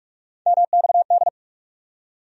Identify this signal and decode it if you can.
Morse code, heard as a keyed tone, about 44 wpm, 700 Hz. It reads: MXD
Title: USA Call Sign Postfixes - 3 Letters 45wpm